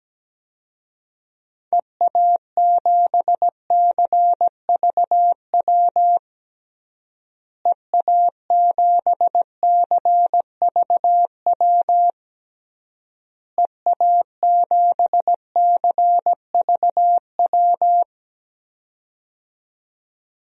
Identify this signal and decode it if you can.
Morse code, heard as a keyed tone, about 17 wpm, 700 Hz. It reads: EA7CVW EA7CVW EA7CVW
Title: DXCC-100 Call Signs - 6 Characters - Repeated 3x - 17wpm